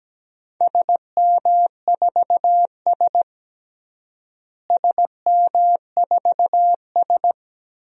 Morse code, heard as a keyed tone, about 17 wpm, 700 Hz. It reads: SM4S SM4S
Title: DXCC-100 Call Signs - 4 Characters - Repeated 3x - 17wpm